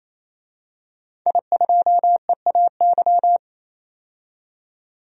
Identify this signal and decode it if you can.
Morse code, heard as a keyed tone, about 27 wpm, 690 Hz. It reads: I2EAY
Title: DXCC-100 Call Signs - 5 Characters - 28wpm